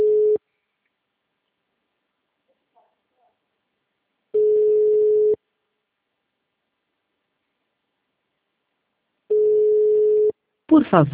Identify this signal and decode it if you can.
Ringback tone (Europe, 425 Hz), with speech.